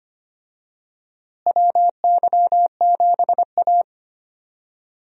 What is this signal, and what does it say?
Morse code, heard as a keyed tone, about 25 wpm, 700 Hz. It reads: WY7A
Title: USA Call Signs - 4 Characters - Repeated 3x - 25wpm